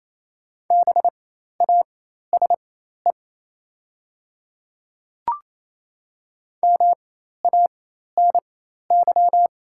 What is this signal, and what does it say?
Morse code, heard as a keyed tone, about 27 wpm, spaced as for 13 wpm, 700 Hz. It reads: BASE MANY
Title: 4x Character Spacing - Top 300 Words - 28wpm